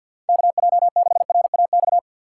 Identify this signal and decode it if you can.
Morse code, heard as a keyed tone, about 53 wpm, 700 Hz. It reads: KJ6RAX
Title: USA Call Signs - 6 Characters - 8x Character Spacing - 50wpm